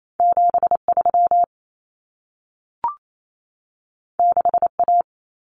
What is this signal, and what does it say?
Morse code, heard as a keyed tone, about 28 wpm, 700 Hz. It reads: 73 6A
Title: Common Cut Numbers - 2-Digit 28wpm